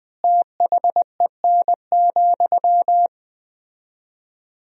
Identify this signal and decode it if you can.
Morse code, heard as a keyed tone, about 20 wpm, 700 Hz. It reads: THEN,
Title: 4x Word-Spacing - Sentences from Top 100 Words - 20wpm